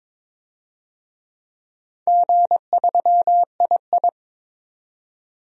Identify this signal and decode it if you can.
Morse code, heard as a keyed tone, about 22 wpm, 700 Hz. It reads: G3II
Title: DXCC-100 Call Signs - 4 Characters - 22wpm